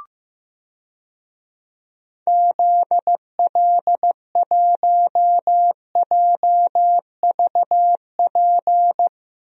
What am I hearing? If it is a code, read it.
Morse code, heard as a keyed tone, about 15 wpm, 700 Hz. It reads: ZL1JVP
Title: DXCC-100 Call Signs - 6 Characters - Repeated 3x - 15wpm